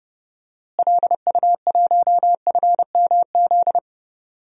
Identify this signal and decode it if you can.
Morse code, heard as a keyed tone, about 30 wpm, 700 Hz. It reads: LU1FMZ
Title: DXCC-100 Call Signs - 6 Characters - Repeated 3x - 30wpm